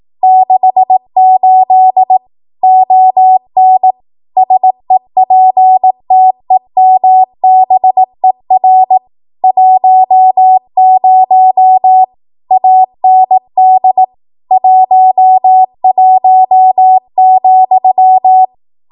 Morse code, heard as a keyed tone, about 18 wpm, 750 Hz. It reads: 68 ON SEPTEMBER 10 AND 11,
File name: Archive / 18 WPM / 18WPM